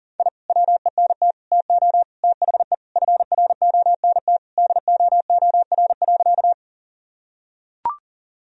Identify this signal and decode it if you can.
Morse code, heard as a keyed tone, about 40 wpm, 700 Hz. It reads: I WENT TO THE FRONT DOOR.
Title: Sentences from Top 1000 Words - Review Part 2 - 40wpm